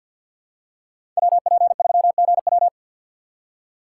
Morse code, heard as a keyed tone, about 47 wpm, 700 Hz. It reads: WP3GW
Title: USA Call Signs - 5 Characters - 2x Character Spacing - 50wpm